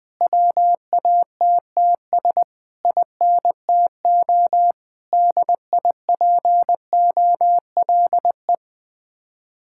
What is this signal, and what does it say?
Morse code, heard as a keyed tone, about 20 wpm, 700 Hz. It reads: WATTS INTO DIPOLE
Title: Speed-Racing Common QSO Phrases - 17wpm